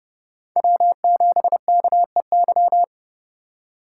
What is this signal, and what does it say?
Morse code, heard as a keyed tone, about 30 wpm, 700 Hz. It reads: W7KEY
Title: USA Call Signs - 5 Characters - 3x Character Spacing - 30wpm